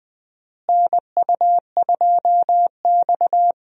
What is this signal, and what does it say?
Morse code, heard as a keyed tone, about 20 wpm, 700 Hz. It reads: NU2X
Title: USA Call Signs - 4 Characters - Repeated 3x - 20wpm